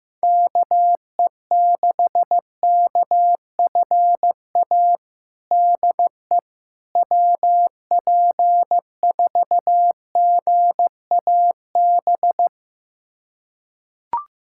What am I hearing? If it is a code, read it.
Morse code, heard as a keyed tone, about 15 wpm, 700 Hz. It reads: KE6KFA DE WP4GAB
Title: USA Callsign DE Callsign - 6 Characters - 15wpm